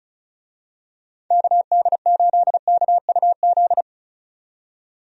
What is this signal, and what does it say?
Morse code, heard as a keyed tone, about 36 wpm, 700 Hz. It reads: KD8KUZ
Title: USA Call Signs - 6 Characters - 35wpm